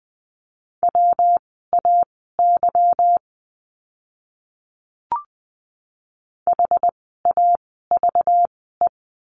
Morse code, heard as a keyed tone, about 20 wpm, spaced as for 14 wpm, 700 Hz. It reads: WAY HAVE
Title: 230 Lesson 37f - Letter x - Words 20wpm